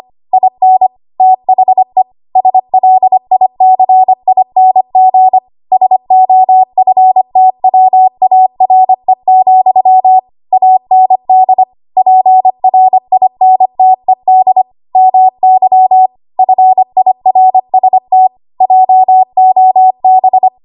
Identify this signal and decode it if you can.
Morse code, heard as a keyed tone, about 25 wpm, 750 Hz. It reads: IN THE SLICING SOFTWARE, AND PRINTED MY FIRST JOB
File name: Archive / 25 WPM / 25WPM